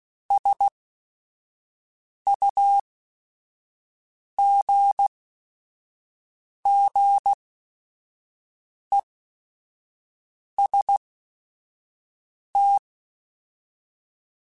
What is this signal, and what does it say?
Morse code, heard as a keyed tone, about 16 wpm, spaced as for 5 wpm, 800 Hz. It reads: SUGGEST
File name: Technical Subjects / Technical Subjects-mp3s / AND Gate @5